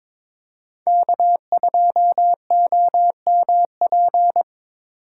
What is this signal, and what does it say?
Morse code, heard as a keyed tone, about 27 wpm, 700 Hz. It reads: K2OMP
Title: Speed Racing - USA 1x3 Call Signs 22wpm